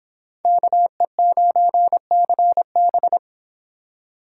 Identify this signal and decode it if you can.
Morse code, heard as a keyed tone, about 25 wpm, 700 Hz. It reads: KE9CB